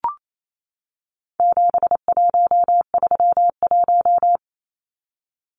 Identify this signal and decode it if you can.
Morse code, heard as a keyed tone, about 28 wpm, 700 Hz. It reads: 7131
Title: Numbers - 4-Digit 28wpm